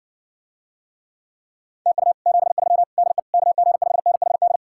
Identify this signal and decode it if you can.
Morse code, heard as a keyed tone, about 60 wpm, 700 Hz. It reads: TU 73 DE KG5NHD